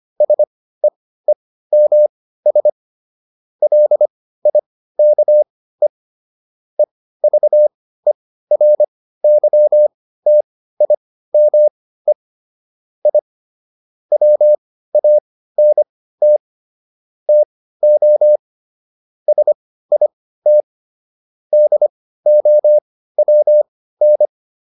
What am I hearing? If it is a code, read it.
Morse code, heard as a keyed tone, about 25 wpm, spaced as for 15 wpm, 600 Hz. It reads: SEEMS LIKE EVERYTIME I WANT TO SIT DOWN